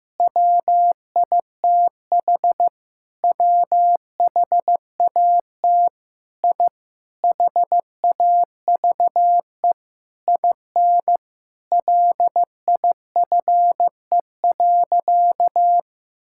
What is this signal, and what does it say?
Morse code, heard as a keyed tone, about 15 wpm, 700 Hz. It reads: WITH WHAT I HAVE IN LIFE.
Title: Sentences from Top 1000 Words - Review Part 6 - 15wpm